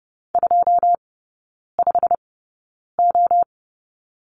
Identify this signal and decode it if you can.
Morse code, heard as a keyed tone, about 30 wpm, spaced as for 9 wpm, 690 Hz. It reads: 25O
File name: Lesson 16e - Letter c - Three Characters 30wpm